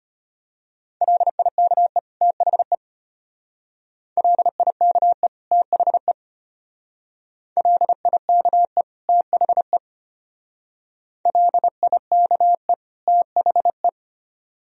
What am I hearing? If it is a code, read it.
Morse code, heard as a keyed tone, about 32 wpm, 700 Hz. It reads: LIKE THE LIKE THE LIKE THE LIKE THE